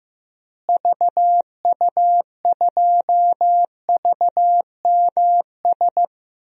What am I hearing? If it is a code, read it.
Morse code, heard as a keyed tone, about 16 wpm, 700 Hz. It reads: VU2VMS
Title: DXCC-100 Call Signs - 6 Characters - Repeated 2x - 15wpm